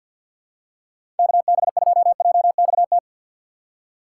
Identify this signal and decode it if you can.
Morse code, heard as a keyed tone, about 53 wpm, 700 Hz. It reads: KB2JXT
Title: USA Call Signs - 6x Character Spacing - 50wpm